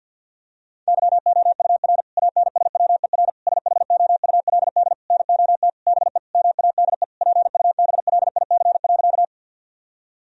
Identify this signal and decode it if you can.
Morse code, heard as a keyed tone, about 50 wpm, 700 Hz. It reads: YOUR ANSWER SHOULD NOT BE MADE PUBLIC.
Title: Sentences from Top 1000 Words - Review Part 3 - 50wpm